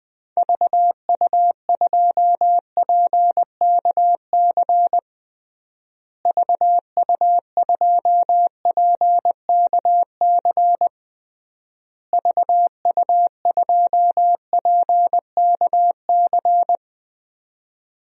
Morse code, heard as a keyed tone, about 20 wpm, 700 Hz. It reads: VU2PKC VU2PKC VU2PKC
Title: DXCC-100 Call Signs - 6 Characters - Repeated 3x - 20wpm